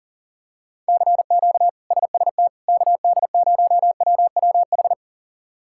Morse code, heard as a keyed tone, about 40 wpm, 700 Hz. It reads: CQ SST KD0WWH
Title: K1USN SST QSO Practice - 40wpm